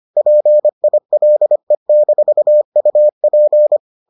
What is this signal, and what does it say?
Morse code, heard as a keyed tone, about 25 wpm, 600 Hz. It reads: PILE-UP